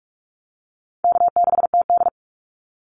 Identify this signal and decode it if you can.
Morse code, heard as a keyed tone, about 46 wpm, 700 Hz. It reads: K6TD
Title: Easy Call Signs - 45wpm